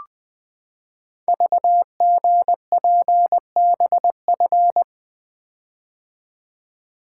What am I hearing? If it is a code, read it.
Morse code, heard as a keyed tone, about 20 wpm, 700 Hz. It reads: VGPBF
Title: Code Groups - 5 Letters - 20wpm